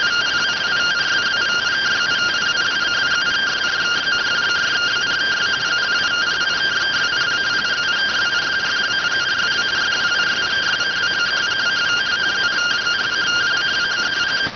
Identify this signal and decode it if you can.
Two-tone teleprinter signal, about 170 Hz shift.